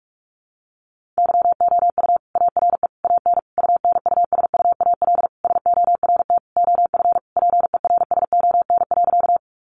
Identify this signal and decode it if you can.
Morse code, heard as a keyed tone, about 44 wpm, 700 Hz. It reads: YOU ARE AN UNUSUAL SORT OF PERSON.
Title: Sentences from Top 1400 Words 45wpm